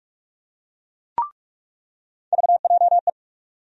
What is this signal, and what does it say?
Morse code, heard as a keyed tone, about 42 wpm, 700 Hz. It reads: VJE
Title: USA Call Sign Postfixes - 3 Letters 45wpm